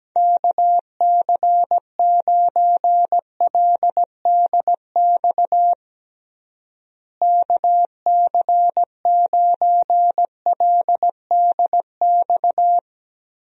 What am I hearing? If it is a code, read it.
Morse code, heard as a keyed tone, about 17 wpm, 700 Hz. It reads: KC9LDX KC9LDX
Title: USA Call Signs - 6 Characters - Repeated 3x - 17wpm